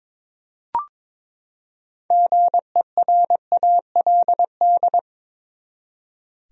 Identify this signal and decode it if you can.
Morse code, heard as a keyed tone, about 22 wpm, 700 Hz. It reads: GERALD
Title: USA Common Names - 22wpm